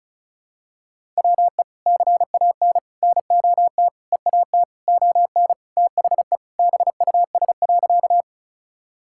Morse code, heard as a keyed tone, about 36 wpm, 700 Hz. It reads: WE CAN NOT EAT ON THE BUS.
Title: Sentences from Top 1000 Words - Review Part 6 - 35wpm